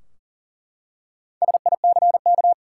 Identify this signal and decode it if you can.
Morse code, heard as a keyed tone, about 40 wpm, 700 Hz. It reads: SICK
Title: Four Letter Words - 40wpm